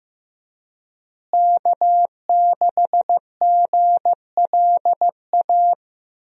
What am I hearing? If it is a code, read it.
Morse code, heard as a keyed tone, about 16 wpm, 700 Hz. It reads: K6GLA
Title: USA Call Signs - 5 Characters - Repeated 3x - 15wpm